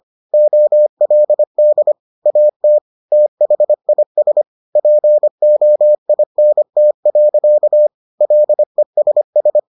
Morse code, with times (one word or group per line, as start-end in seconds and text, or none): 0.3-1.9 OLD
2.3-2.8 AT
3.1-4.4 THIS
4.8-7.9 POINT.
8.2-9.6 LESS